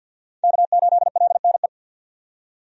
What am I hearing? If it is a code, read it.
Morse code, heard as a keyed tone, about 46 wpm, 700 Hz. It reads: K8LNE
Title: USA Call Signs - 5 Characters - 50wpm